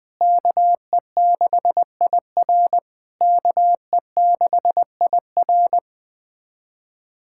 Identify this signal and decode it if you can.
Morse code, heard as a keyed tone, about 20 wpm, 690 Hz. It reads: KE6IR KE6IR
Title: SOTA QSO Practice 20wpm